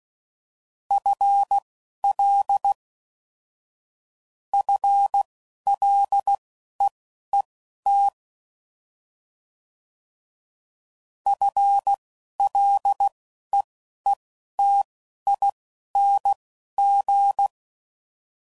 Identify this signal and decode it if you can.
Morse code, heard as a keyed tone, about 16 wpm, spaced as for 11 wpm, 800 Hz. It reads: FL FLEET FLEETING
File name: Mixed / Short Runs / FL Prefixes